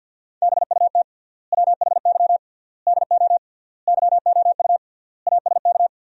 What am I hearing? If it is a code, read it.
Morse code, heard as a keyed tone, about 50 wpm, 700 Hz. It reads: BUT WHY DO YOU ASK
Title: 3x Word-Spacing - Sentences from Top 200 Words - 50wpm